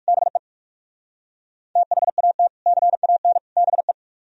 Morse code, heard as a keyed tone, about 55 wpm, 700 Hz. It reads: BE THAT CAN BE
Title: Speed-Racing - Sets of 3 Words 45wpm